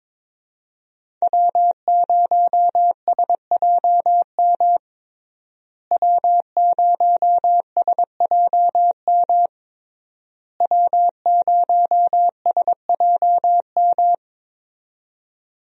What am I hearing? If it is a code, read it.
Morse code, heard as a keyed tone, about 23 wpm, 700 Hz. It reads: W0SJM W0SJM W0SJM
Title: USA Call Signs - 5 Characters - Repeated 3x - 22wpm